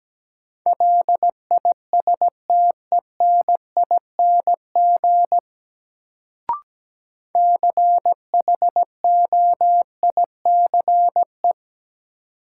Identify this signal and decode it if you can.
Morse code, heard as a keyed tone, about 17 wpm, 700 Hz. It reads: LISTENING CHOICE
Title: Sets of 5 Words - New Word Drill - 17wpm